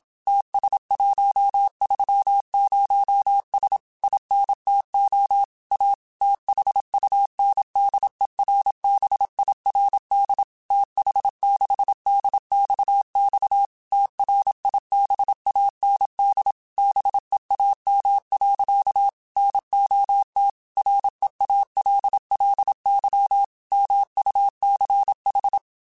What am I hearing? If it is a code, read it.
Morse code, heard as a keyed tone, about 27 wpm, 800 Hz. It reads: TS130S INTO A THUNDERBIRD TH6DXX TRIBAND BEAM. NOT REALLY MUCH
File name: QSOs / QSOs-mp3s / WD0FBT 1@25